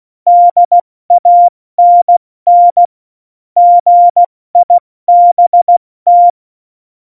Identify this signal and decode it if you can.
Morse code, heard as a keyed tone, about 16 wpm, 700 Hz. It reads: DANN GIBT
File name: Morsekurs / Witze / Witz 11. September 2018, 70 F80